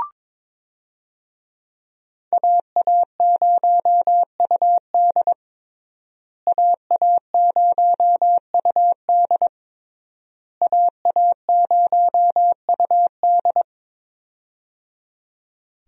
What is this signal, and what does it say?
Morse code, heard as a keyed tone, about 22 wpm, 700 Hz. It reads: AA0UD AA0UD AA0UD
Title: USA Call Signs - 5 Characters - Repeated 3x - 22wpm